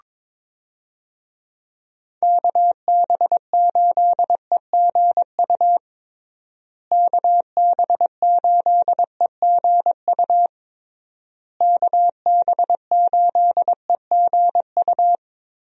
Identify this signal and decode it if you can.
Morse code, heard as a keyed tone, about 22 wpm, 700 Hz. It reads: KB8EGU KB8EGU KB8EGU
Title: USA Call Signs - 6 Characters - Repeated 3x - 22wpm